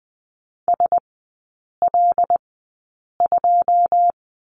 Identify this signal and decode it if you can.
Morse code, heard as a keyed tone, about 20 wpm, spaced as for 8 wpm, 690 Hz. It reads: SL2
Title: 081 Lesson 14e - Number 2 - Three Characters 20wpm